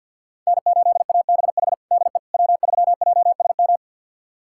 Morse code, heard as a keyed tone, about 51 wpm, 700 Hz. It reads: N8ABH DE W3JSM
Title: USA Callsign DE Callsign - 5 Characters - 50wpm